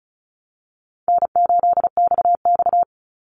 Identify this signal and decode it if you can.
Morse code, heard as a keyed tone, about 34 wpm, 700 Hz. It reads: N8XX
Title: Easy Call Signs Repeated 2x - 35wpm